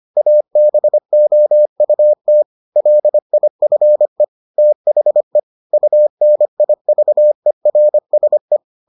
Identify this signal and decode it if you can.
Morse code, heard as a keyed tone, about 25 wpm, 600 Hz. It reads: ABOUT LIFE THE UNIVERSE